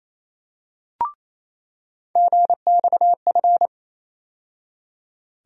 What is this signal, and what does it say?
Morse code, heard as a keyed tone, about 28 wpm, 700 Hz. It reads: GXF
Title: USA Call Sign Postfixes - 3 Letters 28wpm